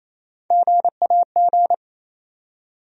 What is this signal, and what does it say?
Morse code, heard as a keyed tone, about 28 wpm, 700 Hz. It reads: GAG
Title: USA Call Sign Postfixes - 3 Letters 28wpm